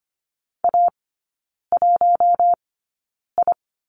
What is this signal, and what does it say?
Morse code, heard as a keyed tone, about 25 wpm, spaced as for 9 wpm, 700 Hz. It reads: A1I
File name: Lesson 14e - Number 2 - Three Characters 25wpm